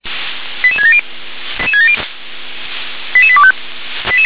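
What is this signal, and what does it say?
Five-tone selective call (ZVEI), heard as quick runs of five tones, tones of about 70 ms. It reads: 88670 88825
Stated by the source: Five-tone ZVEI selective call received by SDR.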